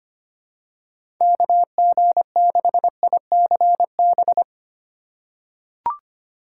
Morse code, heard as a keyed tone, about 25 wpm, 700 Hz. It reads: KG6ICB
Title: USA Call Signs - 6 Characters - 4x Character Spacing - 25wpm